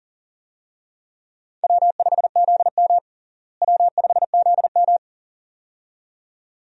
Morse code, heard as a keyed tone, about 40 wpm, 700 Hz. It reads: W5ZM W5ZM
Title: USA Call Signs - 4 Characters - Repeated 2x - 40wpm